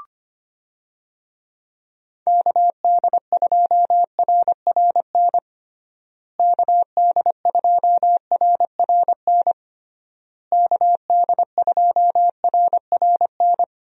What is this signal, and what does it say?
Morse code, heard as a keyed tone, about 25 wpm, 700 Hz. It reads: KD2RRN KD2RRN KD2RRN
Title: USA Call Signs - 6 Characters - Repeated 3x - 25wpm